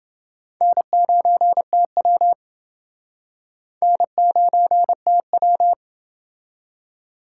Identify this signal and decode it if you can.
Morse code, heard as a keyed tone, about 26 wpm, 700 Hz. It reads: N9TW N9TW